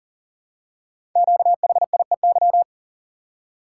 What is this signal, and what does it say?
Morse code, heard as a keyed tone, about 41 wpm, 700 Hz. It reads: QHIEY
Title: Code Groups - 5 Letters - 40wpm